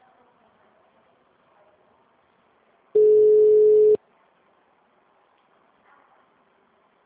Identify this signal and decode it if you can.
Ringback tone (Europe, 425 Hz).